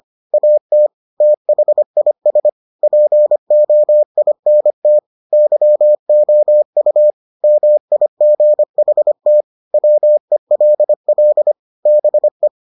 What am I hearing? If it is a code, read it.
Morse code, heard as a keyed tone, about 25 wpm, 600 Hz. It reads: AT THIS POINT YOU MIGHT WELL BE